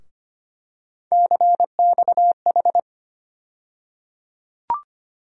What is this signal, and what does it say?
Morse code, heard as a keyed tone, about 25 wpm, 700 Hz. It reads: CXH
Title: USA Call Sign Postfixes - 3 Letters 25wpm